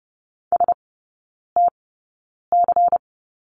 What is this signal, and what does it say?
Morse code, heard as a keyed tone, about 30 wpm, spaced as for 9 wpm, 700 Hz. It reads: STC
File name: Lesson 16e - Letter c - Three Characters 30wpm